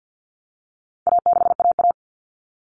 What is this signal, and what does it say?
Morse code, heard as a keyed tone, about 50 wpm, 700 Hz. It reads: A6AA
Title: All Cut Numbers - 4-Digit 50wpm